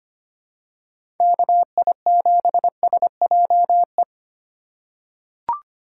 Morse code, heard as a keyed tone, about 25 wpm, 700 Hz. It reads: KI7SJE
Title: USA Call Signs - 6 Characters - 2x Character Spacing - 25wpm